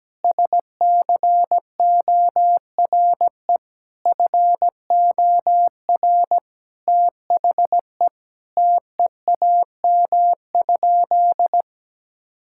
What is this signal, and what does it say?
Morse code, heard as a keyed tone, about 17 wpm, 700 Hz. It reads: SCORE FOR THE TEAM?
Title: Sentences from Top 1600 Words 17wpm